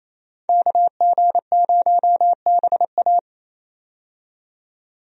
Morse code, heard as a keyed tone, about 28 wpm, 700 Hz. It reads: KG0BA
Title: USA Call Signs - 5 Characters - 28wpm